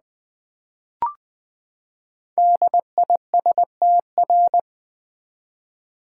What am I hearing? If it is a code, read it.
Morse code, heard as a keyed tone, about 20 wpm, 700 Hz. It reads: DISTR
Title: Build A Word - 5 Letters At A Time - 12-Letter Words - 20wpm